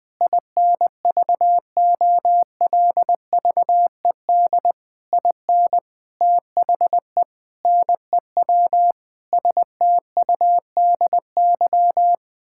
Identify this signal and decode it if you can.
Morse code, heard as a keyed tone, about 21 wpm, 700 Hz. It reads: INVOLVED IN THE NEW STUDY
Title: Sets of 8 Words - 20wpm